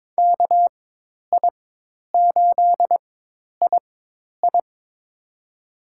Morse code, heard as a keyed tone, about 22 wpm, spaced as for 10 wpm, 700 Hz. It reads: KI8II
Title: USA Call Signs - 5 Characters - 4x Character Spacing - 22wpm